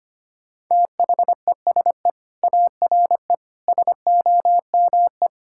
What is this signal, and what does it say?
Morse code, heard as a keyed tone, about 25 wpm, 700 Hz. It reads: THESE ARE SOME
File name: practice / 25wpm / Sentences with 4-Word Phrases - 25wpm